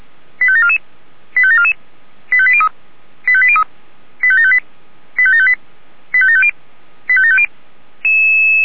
Five-tone selective call (ZVEI), heard as quick runs of five tones, tones of about 70 ms. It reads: 85744 85744 85803 85803 85758 85758 85750 85750
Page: /464/SDR/ZVEI/